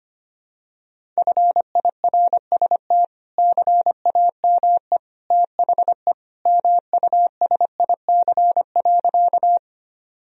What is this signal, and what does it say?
Morse code, heard as a keyed tone, about 25 wpm, 700 Hz. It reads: FIRST CAME THE MUSIC.
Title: Sentences from Top 1000 Words - Review Part 1 - 25wpm